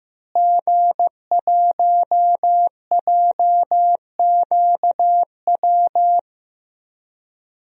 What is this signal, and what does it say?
Morse code, heard as a keyed tone, about 15 wpm, 700 Hz. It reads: G1JQW
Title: DXCC-100 Call Signs - 5 Characters - Repeated 3x - 15wpm